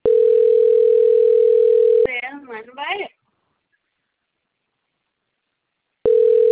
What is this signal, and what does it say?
Ringback tone (North America), 440 + 480 Hz, with speech.
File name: rg-1600-unknown- - - .80287